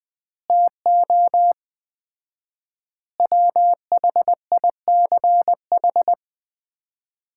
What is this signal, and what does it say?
Morse code, heard as a keyed tone, about 20 wpm, 700 Hz. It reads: TO WHICH
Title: 4x Word-Spacing - Sets of 4 Words - 20wpm